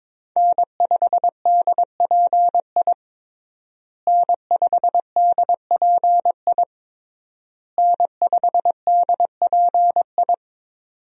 Morse code, heard as a keyed tone, about 22 wpm, 700 Hz. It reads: N5DPI N5DPI N5DPI
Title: USA Call Signs - 5 Characters - Repeated 3x - 22wpm